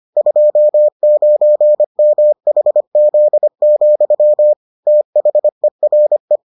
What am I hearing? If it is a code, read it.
Morse code, heard as a keyed tone, about 25 wpm, 600 Hz. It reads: 29MHZ, THERE